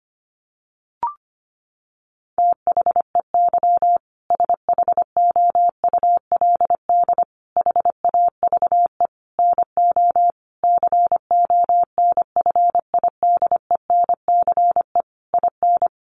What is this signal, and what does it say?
Morse code, heard as a keyed tone, about 25 wpm, 700 Hz. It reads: THEY SHOULD HAVE NO CONFIDENCE IN
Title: Sentences from Top 1500 Words 25wpm